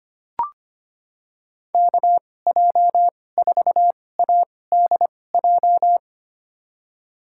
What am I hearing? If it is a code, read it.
Morse code, heard as a keyed tone, about 25 wpm, spaced as for 18 wpm, 700 Hz. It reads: KJ4ADJ
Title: USA Call Signs - 6 Characters - 2x Character Spacing - 25wpm